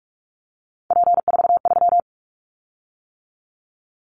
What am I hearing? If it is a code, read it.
Morse code, heard as a keyed tone, about 46 wpm, 700 Hz. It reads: P43
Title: Callsign Prefixes for Top 100 DXCC - 45wpm